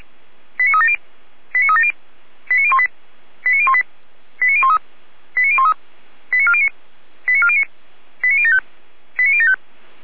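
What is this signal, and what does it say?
Five-tone selective call (ZVEI), heard as quick runs of five tones, tones of about 70 ms. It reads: 89370 89370 89018 89018 89013 89013 89409 89409 89075 89075
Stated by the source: ZVEI five-tone selective call received on an SDR, archived by fire brigade Orlamünde (feuerwehr-orlamuende).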